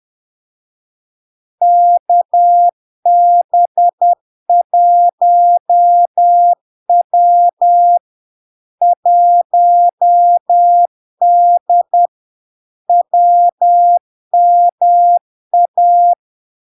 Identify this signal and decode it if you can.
Morse code, heard as a keyed tone, about 10 wpm, 700 Hz. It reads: KB1W 1D WMA
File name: LARK-FD-2022-Log-in-CW-10wpm-700-Hz